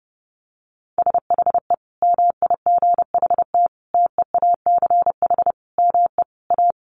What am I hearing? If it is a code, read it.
Morse code, heard as a keyed tone, about 30 wpm, 700 Hz. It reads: SHE MIGHT TEACH ME A
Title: Sentences from Top 1200 Words 30wpm